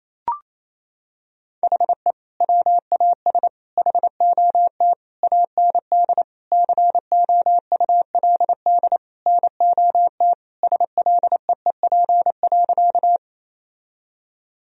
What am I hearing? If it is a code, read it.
Morse code, heard as a keyed tone, about 28 wpm, 700 Hz. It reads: HE WAS HOT AND COULD NOT SLEEP.
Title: Sentences from Top 1000 Words - Review Part 3 - 28wpm